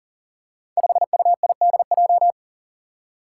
Morse code, heard as a keyed tone, about 40 wpm, 700 Hz. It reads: 5UIDJ
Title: Five Challenging Letter-Number Combinations 40wpm